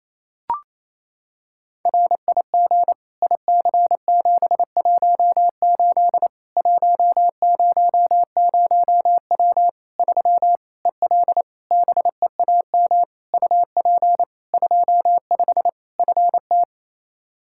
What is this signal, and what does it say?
Morse code, heard as a keyed tone, about 28 wpm, 700 Hz. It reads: RIG IC718 100W 3 EL BEAM UP 25 FT
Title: Practice QSO - Exchange 2 - Info - 28wpm